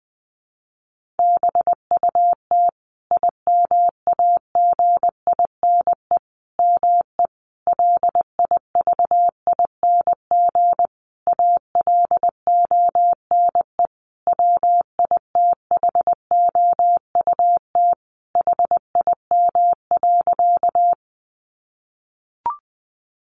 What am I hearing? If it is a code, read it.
Morse code, heard as a keyed tone, about 20 wpm, 700 Hz. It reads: BUT IMAGINE ME LIVING ALONE WITHOUT HIM.
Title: Sentences from Top 1400 Words 20wpm